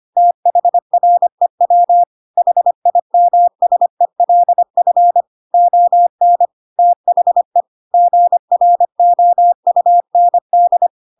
Morse code, heard as a keyed tone, about 25 wpm, 700 Hz. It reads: THREW HIMSELF ON THE GROUND
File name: Pinocchio / Pinocchio ch3 25 25 ik1hge